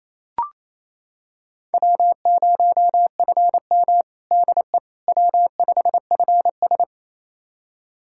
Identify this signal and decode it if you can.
Morse code, heard as a keyed tone, about 29 wpm, 700 Hz. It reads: W0FM DE W5FS
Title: USA Callsign DE Callsign - 4 Characters - 28wpm